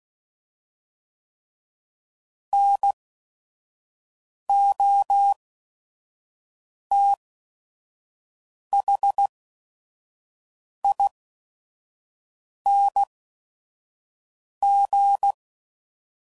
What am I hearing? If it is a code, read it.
Morse code, heard as a keyed tone, about 16 wpm, spaced as for 5 wpm, 800 Hz. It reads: NOTHING
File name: Suffixes / Special ING's / VerySlowToFaster ING's